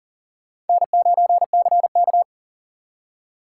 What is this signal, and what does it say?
Morse code, heard as a keyed tone, about 51 wpm, 700 Hz. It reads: N9CK